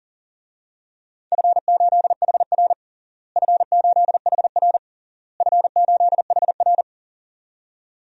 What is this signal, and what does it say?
Morse code, heard as a keyed tone, about 40 wpm, 700 Hz. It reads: F8HR F8HR F8HR
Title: DXCC-100 Call Signs - 4 Characters - Repeated 3x - 40wpm